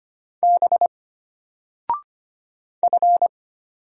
Morse code, heard as a keyed tone, about 25 wpm, 700 Hz. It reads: B F
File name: Single-Letter-Number-25wpm